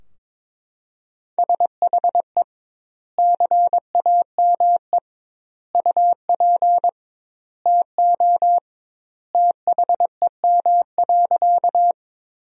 Morse code, heard as a keyed tone, about 22 wpm, 700 Hz. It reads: SHE CAME UP TO THEM.
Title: 2x Word-Spacing - Sentences from Top 200 Words - 22wpm